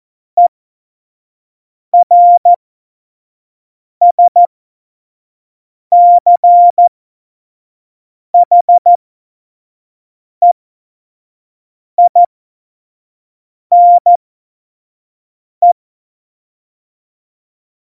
Morse code, heard as a keyed tone, about 14 wpm, spaced as for 5 wpm, 700 Hz. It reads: ERSCHEINE
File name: Morsekurs / Witze / Witz 21 Juli 2020, 25 F70